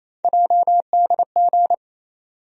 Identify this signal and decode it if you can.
Morse code, heard as a keyed tone, about 28 wpm, 700 Hz. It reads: JDG
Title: USA Call Sign Postfixes - 3 Letters 28wpm